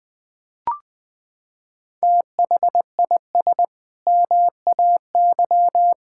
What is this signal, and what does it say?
Morse code, heard as a keyed tone, about 20 wpm, 700 Hz. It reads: THIS MAY
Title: Sentences with 8-Word Phrases - 20wpm